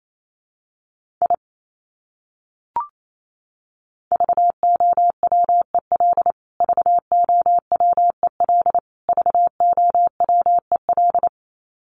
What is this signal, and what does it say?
Morse code, heard as a keyed tone, about 28 wpm, 700 Hz. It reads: I VOWEL VOWEL VOWEL
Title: Top 500 Words Repeated 28wpm